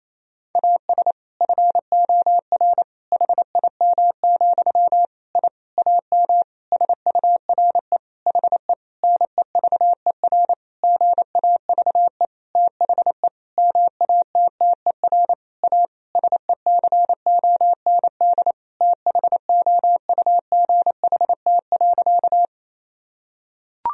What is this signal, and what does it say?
Morse code, heard as a keyed tone, about 28 wpm, 700 Hz. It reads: AS FOR HIM, I AM SURE HE NEVER GAVE THE MATTER A SECOND THOUGHT.
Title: Sentences from Top 1000 Words - Review Part 2 - 28wpm